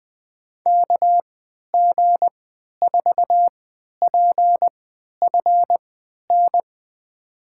Morse code, heard as a keyed tone, about 20 wpm, spaced as for 11 wpm, 700 Hz. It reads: KG4PFN
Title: USA Call Signs - 6 Characters - 3x Character Spacing - 20wpm